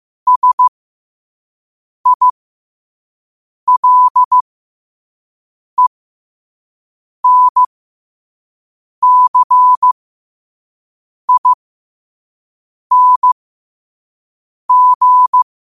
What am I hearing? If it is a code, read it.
Morse code, heard as a keyed tone, about 15 wpm, spaced as for 5 wpm, 1000 Hz. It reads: SILENCING
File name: 5wpm